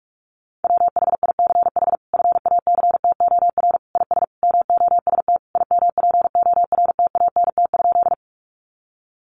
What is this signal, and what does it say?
Morse code, heard as a keyed tone, about 44 wpm, 700 Hz. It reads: WHICH FACTOR IS MOST IMPORTANT?
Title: Sentences from Top 1300 Words 45wpm